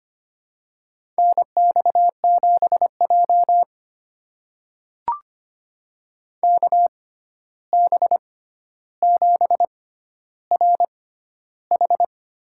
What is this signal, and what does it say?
Morse code, heard as a keyed tone, about 25 wpm, spaced as for 12 wpm, 700 Hz. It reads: NX7J KB7RH
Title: USA Call Signs - 6x Character Spacing - 25wpm